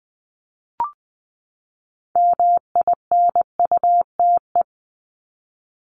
Morse code, heard as a keyed tone, about 20 wpm, 700 Hz. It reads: MINUTE
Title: Top 500 Words 20wpm